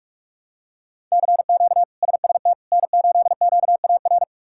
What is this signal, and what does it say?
Morse code, heard as a keyed tone, about 41 wpm, 700 Hz. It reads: CQ SST N8QAR
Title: K1USN SST QSO Practice - 45wpm